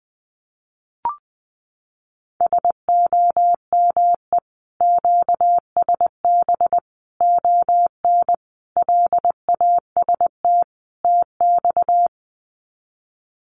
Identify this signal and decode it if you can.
Morse code, heard as a keyed tone, about 19 wpm, 700 Hz. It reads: SOME QSB ON LAST TX
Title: Common QSO Phrases 20wpm